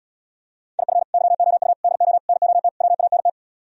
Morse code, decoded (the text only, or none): IF YOU DO NOT WANT